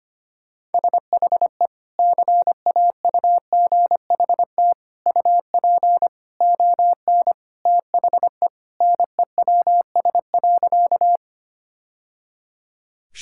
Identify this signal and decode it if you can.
Morse code, heard as a keyed tone, about 25 wpm, 700 Hz. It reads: SHE CAUGHT UP ON THE NEWS.
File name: practice / 25wpm / Sentences with 3-Word Phrases - 25wpm